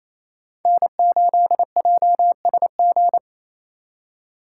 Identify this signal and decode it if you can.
Morse code, heard as a keyed tone, about 28 wpm, 700 Hz. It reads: N8JSG
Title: USA Call Signs - 5 Characters - 2x Character Spacing - 28wpm